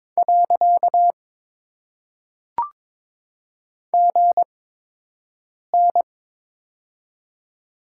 Morse code, heard as a keyed tone, about 22 wpm, 700 Hz. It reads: . G N
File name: Top 90 QSO Elements - 8x Character Spacing - 22wpm